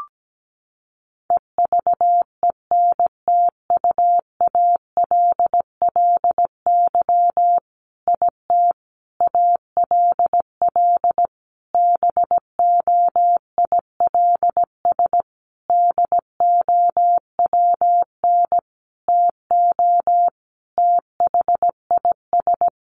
Morse code, heard as a keyed tone, about 17 wpm, 700 Hz. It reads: EVENTUALLY IT ALL BOILS DOWN TO THIS
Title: A Sound Of Thunder 17wpm